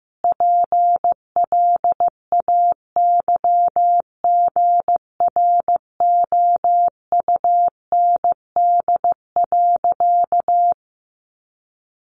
Morse code, heard as a keyed tone, about 15 wpm, 700 Hz. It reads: PLAYGROUND.
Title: Additional Easy Phrases 15wpm